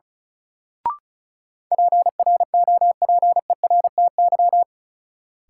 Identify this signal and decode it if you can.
Morse code, heard as a keyed tone, about 36 wpm, 700 Hz. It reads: PROPERTY
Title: Sentences from Top 1000 Words - New Word Drill - 35wpm